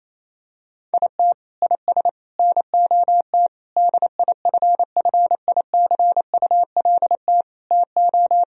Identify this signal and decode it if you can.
Morse code, heard as a keyed tone, about 28 wpm, 700 Hz. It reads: IT IS NOT DIFFICULT TO
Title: Sentences from Top 1000 Words - Review Part 4 - 28wpm